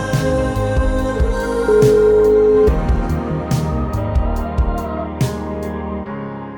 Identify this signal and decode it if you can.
Ringback tone (Europe, 425 Hz), with noise.